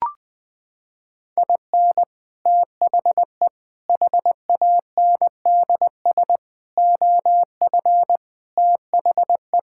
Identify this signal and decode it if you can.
Morse code, heard as a keyed tone, about 19 wpm, 700 Hz. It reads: IN THE HANDS OF THE
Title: Sentences with 5-Word Phrases - 20wpm